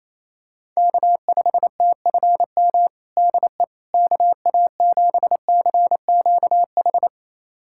Morse code, heard as a keyed tone, about 28 wpm, 700 Hz. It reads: K5TFM DE KA7CQH
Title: USA Callsign DE Callsign 28wpm